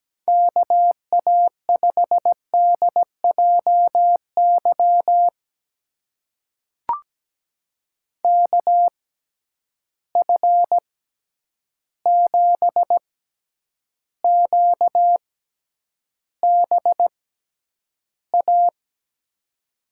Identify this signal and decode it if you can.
Morse code, heard as a keyed tone, about 17 wpm, spaced as for 9 wpm, 700 Hz. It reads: KA5DJY KF7QBA
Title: USA Call Signs - 6x Character Spacing - 17wpm